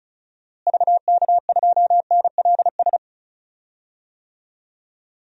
Morse code, heard as a keyed tone, about 36 wpm, 700 Hz. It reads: VK2NLS
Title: DXCC-100 Call Signs - 6 Characters - Repeated 3x - 35wpm